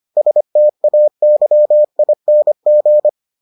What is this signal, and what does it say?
Morse code, heard as a keyed tone, about 25 wpm, 600 Hz. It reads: STAYING